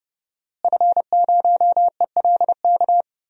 Morse code, heard as a keyed tone, about 30 wpm, 700 Hz. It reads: F0ELK
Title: DXCC-100 Call Signs - 5 Characters - Repeated 2x - 30wpm